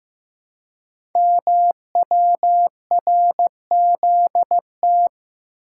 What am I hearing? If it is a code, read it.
Morse code, heard as a keyed tone, about 15 wpm, 700 Hz. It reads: MWRZT
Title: Code Groups - 5 Letters - 15wpm